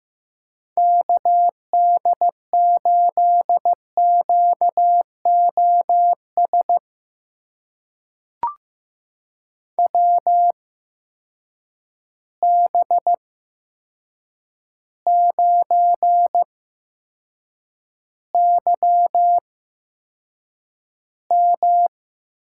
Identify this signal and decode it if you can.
Morse code, heard as a keyed tone, about 15 wpm, spaced as for 7 wpm, 700 Hz. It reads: KD8QOS WB9YM
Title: USA Call Signs - 6 Characters - 8x Character Spacing - 15wpm